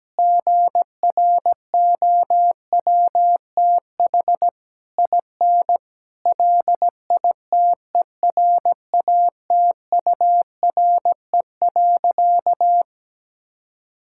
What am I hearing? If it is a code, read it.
Morse code, heard as a keyed tone, about 17 wpm, 700 Hz. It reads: GROWTH IN LITERATURE.
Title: Sentences from Top 1000 Words - Review Part 5 - 17wpm